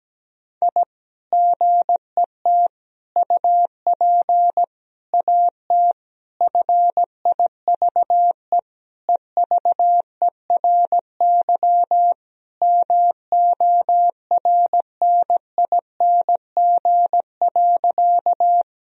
Morse code, heard as a keyed tone, about 17 wpm, 700 Hz. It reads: I GET UP AT FIVE EVERY MORNING.
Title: Sentences from Top 1000 Words - Review Part 2 - 17wpm